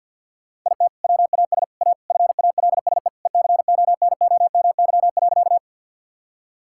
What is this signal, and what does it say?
Morse code, heard as a keyed tone, about 50 wpm, 700 Hz. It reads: IT WAS A FALSE ECONOMY.
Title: Sentences from Top 1600 Words 50wpm